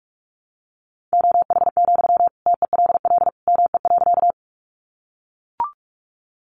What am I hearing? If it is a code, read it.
Morse code, heard as a keyed tone, about 44 wpm, 700 Hz. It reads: OH, TELL ME.
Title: Sentences from Top 500 Words 45wpm